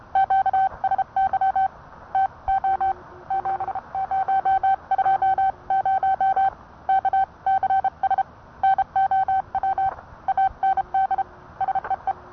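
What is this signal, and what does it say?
Morse code, heard as a keyed tone, about 30 wpm, 770 Hz. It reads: QSY TO 7020 KCS NOW AND SEE
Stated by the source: Morse code sent by N1EA on a Vibroplex key.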